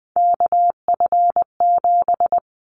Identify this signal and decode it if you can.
Morse code, heard as a keyed tone, about 20 wpm, 700 Hz. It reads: KF7
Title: Callsign Prefixes for Top 25 DXCC - Repeated 3x - 20wpm